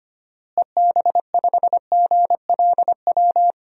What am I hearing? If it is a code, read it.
Morse code, heard as a keyed tone, about 25 wpm, 700 Hz. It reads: EB5GLW
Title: DXCC-100 Call Signs - 6 Characters - Repeated 3x - 25wpm